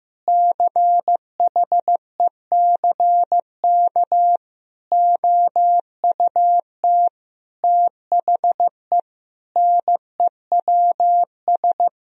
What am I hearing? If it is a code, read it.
Morse code, heard as a keyed tone, about 15 wpm, 700 Hz. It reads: CHECK OUT THE NEWS
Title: Sentences from Top 1000 Words - Review Part 4 - 15wpm